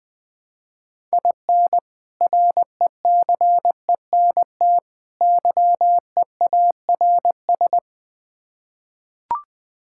Morse code, heard as a keyed tone, about 20 wpm, 700 Hz. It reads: IN RECENT YEARS
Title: Build A Phrase - 3-Word Phrases - 20wpm